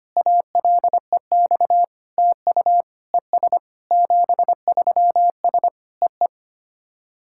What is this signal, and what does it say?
Morse code, heard as a keyed tone, about 25 wpm, 700 Hz. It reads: ALEX TU ES 73S EE
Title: SOTA QSO Practice 25wpm